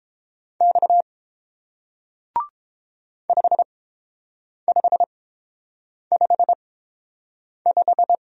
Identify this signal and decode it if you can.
Morse code, heard as a keyed tone, about 30 wpm, spaced as for 7 wpm, 700 Hz. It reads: X 5555